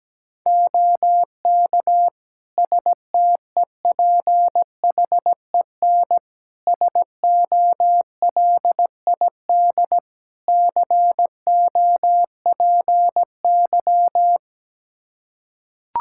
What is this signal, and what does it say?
Morse code, heard as a keyed tone, about 17 wpm, 700 Hz. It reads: OK STEPHEN SOLID COPY
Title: K1USN SST QSO Practice 17wpm